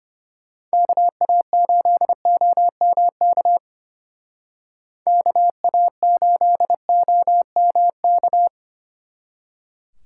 Morse code, heard as a keyed tone, about 31 wpm, 700 Hz. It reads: KA8OMK KA8OMK
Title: Speed Racing - USA 2x3 Call Signs 25wpm